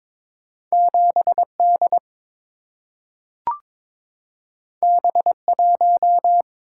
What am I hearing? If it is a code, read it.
Morse code, heard as a keyed tone, about 22 wpm, 700 Hz. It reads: 7D B1
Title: Single-Letter Single-Number 22wpm